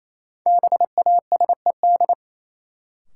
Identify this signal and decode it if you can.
Morse code, heard as a keyed tone, about 28 wpm, 700 Hz. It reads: BASED
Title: Sets of 2 Words - New Word Drill - 28wpm